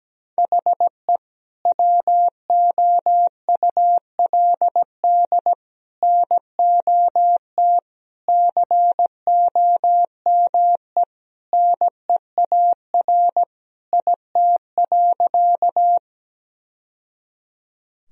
Morse code, heard as a keyed tone, about 17 wpm, 700 Hz. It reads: HE WOULD NOT COME NEAR IT.
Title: Sentences from Top 1000 Words - Review Part 1 - 17wpm